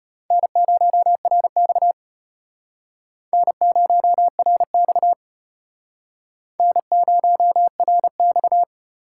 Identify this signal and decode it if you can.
Morse code, heard as a keyed tone, about 34 wpm, 700 Hz. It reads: N0RX N0RX N0RX